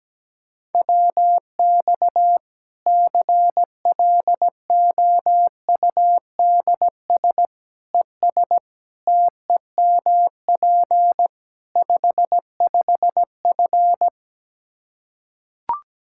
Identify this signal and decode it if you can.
Morse code, heard as a keyed tone, about 17 wpm, 700 Hz. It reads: WX CLOUDS ES TEMP 55F
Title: Practice QSO - Exchange 2 - Info - 17wpm